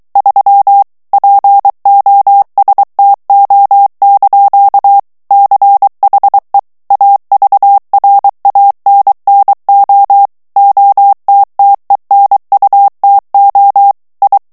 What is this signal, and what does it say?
Morse code, heard as a keyed tone, about 23 wpm, 800 Hz. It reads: 3 POSTO) CHE AVRANNO OTTENUTO I
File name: ITALIEN texte clair 2 vitesse 22 mots minute